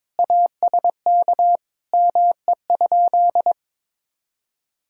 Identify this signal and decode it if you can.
Morse code, heard as a keyed tone, about 22 wpm, 700 Hz. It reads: ASK ME?
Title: Sentences from Top 200 Words and Abbreviations 22wpm